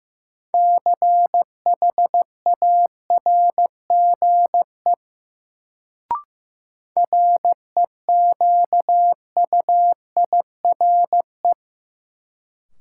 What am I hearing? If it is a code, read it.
Morse code, heard as a keyed tone, about 15 wpm, 700 Hz. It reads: CHARGE REQUIRE
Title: Sentences from Top 900 Words - New Word Drill - 15wpm